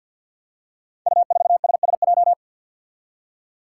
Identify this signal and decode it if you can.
Morse code, heard as a keyed tone, about 52 wpm, 700 Hz. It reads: U4SSJ
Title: Five Challenging Letter-Number Combinations 50wpm